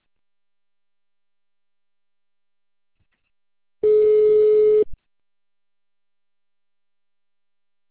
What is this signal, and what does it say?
Ringback tone (Europe, 425 Hz).